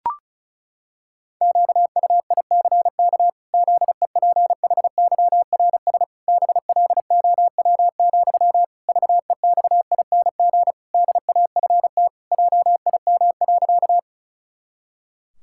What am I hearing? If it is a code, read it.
Morse code, heard as a keyed tone, about 36 wpm, 700 Hz. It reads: QUICK ZEPHYRS BLOW, VEXING DAFT JIM.
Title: Pangram Challenge v2 - 35wpm